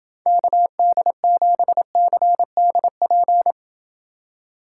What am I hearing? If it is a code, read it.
Morse code, heard as a keyed tone, about 29 wpm, 700 Hz. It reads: KD7CDP